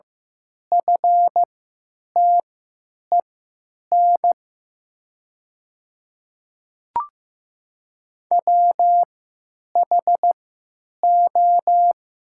Morse code, heard as a keyed tone, about 15 wpm, spaced as for 9 wpm, 700 Hz. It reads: FTEN WHO
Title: 3x Character Spacing - Top 500 Words - 15wpm